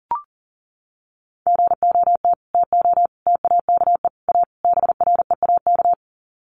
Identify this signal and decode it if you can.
Morse code, heard as a keyed tone, about 40 wpm, 700 Hz. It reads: GOT TO TAKE A BREAK
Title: Sets of 5 Words Encore 40wpm